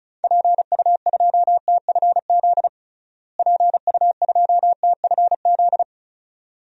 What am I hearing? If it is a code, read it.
Morse code, heard as a keyed tone, about 36 wpm, 700 Hz. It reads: PU2TFZ PU2TFZ
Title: DXCC-100 Call Signs - 6 Characters - Repeated 2x - 35wpm